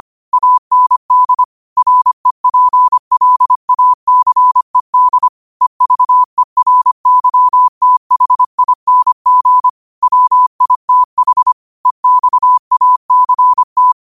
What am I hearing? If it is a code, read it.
Morse code, heard as a keyed tone, about 25 wpm, 1000 Hz. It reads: AND REPLACED EVERYTHING WITH EXACT
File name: qotdfiles / 25wpm